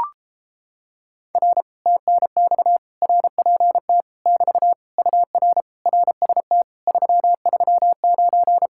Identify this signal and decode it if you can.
Morse code, heard as a keyed tone, about 32 wpm, 700 Hz. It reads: R TNX RPT <BT> UR RST 339